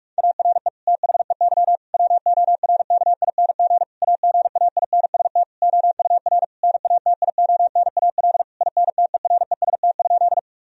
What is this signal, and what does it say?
Morse code, heard as a keyed tone, about 46 wpm, 700 Hz. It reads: ARE THEY WORKING AGAINST OUR NATIONAL INTEREST?
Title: Sentences from Top 1000 Words - Review Part 4 - 45wpm